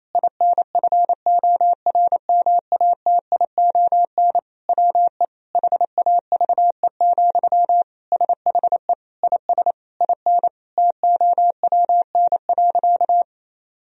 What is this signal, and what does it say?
Morse code, heard as a keyed tone, about 28 wpm, 700 Hz. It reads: INFORMATION WE HAVE, SHE IS IN TOWN.
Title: Sentences from Top 1000 Words - Review Part 4 - 28wpm